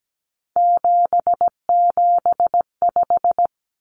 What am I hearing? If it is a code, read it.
Morse code, heard as a keyed tone, about 17 wpm, 700 Hz. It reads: 775
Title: Numbers - 3-Digit 17wpm